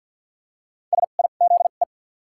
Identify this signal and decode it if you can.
Morse code, heard as a keyed tone, about 48 wpm, 700 Hz. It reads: S I Z E
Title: 2x Character Spacing - Top 500 Words - 50wpm